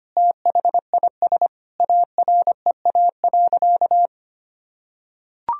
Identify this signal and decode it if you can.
Morse code, heard as a keyed tone, about 25 wpm, 700 Hz. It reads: THIS AREA.
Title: Sentences from Top 1000 Words - Review Part 2 - 25wpm